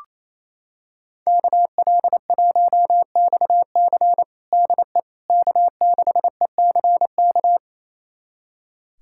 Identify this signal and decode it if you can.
Morse code, heard as a keyed tone, about 28 wpm, 700 Hz. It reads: KL1XC DE K6ECK
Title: USA Callsign DE Callsign - 5 Characters - 28wpm